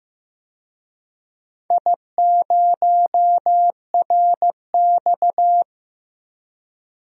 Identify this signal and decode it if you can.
Morse code, heard as a keyed tone, about 15 wpm, 700 Hz. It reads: I0RX
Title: DXCC-100 Call Signs - 4 Characters - Repeated 2x - 15wpm